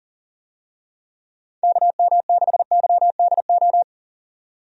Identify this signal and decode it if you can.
Morse code, heard as a keyed tone, about 37 wpm, 700 Hz. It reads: KM6YDO
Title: USA Call Signs - 6 Characters - 40wpm